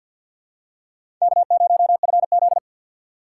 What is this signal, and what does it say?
Morse code, heard as a keyed tone, about 51 wpm, 700 Hz. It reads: K0FZ
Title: USA Call Signs - 4 Characters - 2x Character Spacing - 50wpm